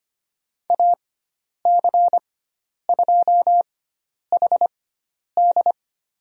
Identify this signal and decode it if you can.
Morse code, heard as a keyed tone, about 25 wpm, spaced as for 10 wpm, 700 Hz. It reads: AC2HD
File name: Lesson 16g - Letter c - Call Signs 25wpm